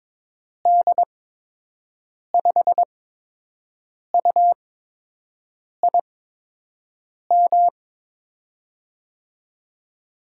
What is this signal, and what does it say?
Morse code, heard as a keyed tone, about 22 wpm, spaced as for 6 wpm, 700 Hz. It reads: D5UIM
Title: USA Call Signs - 6 Characters - 8x Character Spacing - 22wpm